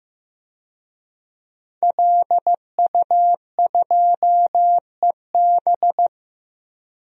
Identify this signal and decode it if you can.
Morse code, heard as a keyed tone, about 15 wpm, 700 Hz. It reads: LU2EB
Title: DXCC-100 Call Signs - 5 Characters - Repeated 3x - 15wpm